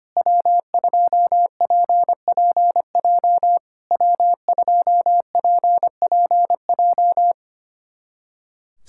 Morse code, heard as a keyed tone, about 25 wpm, 700 Hz. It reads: W2PPJ W2PPJ
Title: SOTA QSO Practice 25wpm